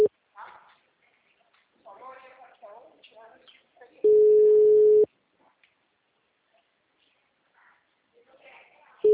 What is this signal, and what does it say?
Ringback tone (Europe, 425 Hz), with speech.